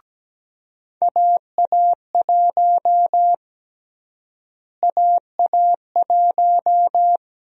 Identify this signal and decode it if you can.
Morse code, heard as a keyed tone, about 17 wpm, 700 Hz. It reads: AA1 AA1
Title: USA Call Sign Prefixes - Repeated 2x - 17wpm